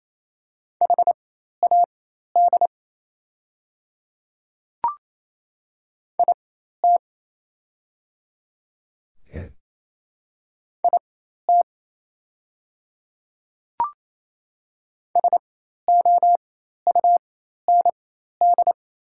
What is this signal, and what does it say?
Morse code, heard as a keyed tone, about 28 wpm, spaced as for 13 wpm, 700 Hz. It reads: HAD IT IT SOUND
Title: 4x Character Spacing - Top 100 Words - 28wpm